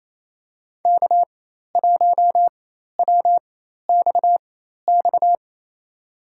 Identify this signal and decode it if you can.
Morse code, heard as a keyed tone, about 28 wpm, spaced as for 13 wpm, 700 Hz. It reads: K1WXX
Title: USA Call Signs - 4x Character Spacing - 28wpm